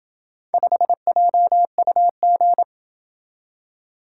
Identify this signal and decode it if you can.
Morse code, heard as a keyed tone, about 26 wpm, 700 Hz. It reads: 5JUG